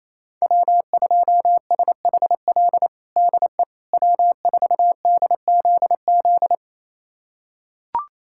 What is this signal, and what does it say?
Morse code, heard as a keyed tone, about 28 wpm, 700 Hz. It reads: W2SHL DE W4DZZ
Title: USA Callsign DE Callsign - 5 Characters - 28wpm